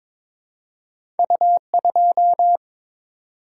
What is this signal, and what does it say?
Morse code, heard as a keyed tone, about 22 wpm, 700 Hz. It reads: U2
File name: practice / 22wpm / Two Letter-Number Combinations - 22wpm